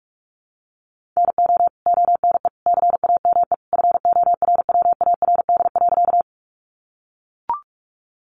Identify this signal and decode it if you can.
Morse code, heard as a keyed tone, about 44 wpm, 700 Hz. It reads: NO ONE CAME FORWARD.
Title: Sentences from Top 1300 Words 45wpm